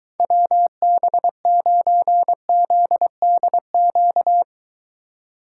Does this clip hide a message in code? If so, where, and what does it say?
Yes: Morse code from 0.2 to 4.4 s, heard as a keyed tone, about 23 wpm, 700 Hz. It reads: WB9ZDQ